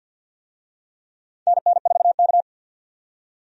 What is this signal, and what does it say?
Morse code, heard as a keyed tone, about 52 wpm, 700 Hz. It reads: NN4K
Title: USA Call Signs - 4 Characters - 50wpm